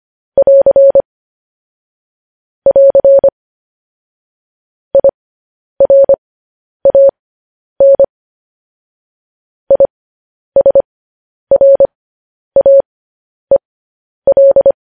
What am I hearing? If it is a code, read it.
Morse code, heard as a keyed tone, about 25 wpm, spaced as for 10 wpm, 550 Hz. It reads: <AR> <AR> IRAN ISRAEL